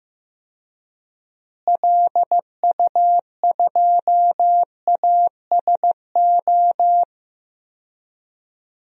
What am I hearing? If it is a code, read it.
Morse code, heard as a keyed tone, about 15 wpm, 700 Hz. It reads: LU2ASO
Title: DXCC-100 Call Signs - 6 Characters - Repeated 3x - 15wpm